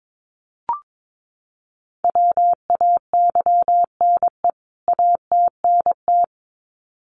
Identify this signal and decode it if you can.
Morse code, heard as a keyed tone, about 22 wpm, 700 Hz. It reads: WAYNE ATNT
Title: CWT Exchange 22wpm